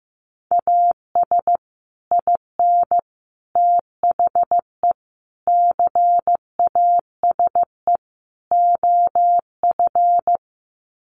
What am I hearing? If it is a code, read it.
Morse code, heard as a keyed tone, about 15 wpm, 700 Hz. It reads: AS IN THE CASE OF
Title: Sets of 5 Words 15wpm